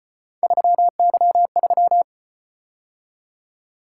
Morse code, heard as a keyed tone, about 34 wpm, 700 Hz. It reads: 3Y3